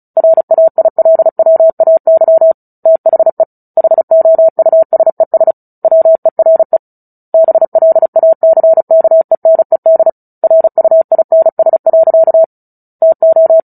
Morse code, heard as a keyed tone, about 34 wpm, 650 Hz. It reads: RAILWAY THE HOUSES WERE BLACKENED RUINS. TO
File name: Book Two / 09-Chapter 9 - 35 WPM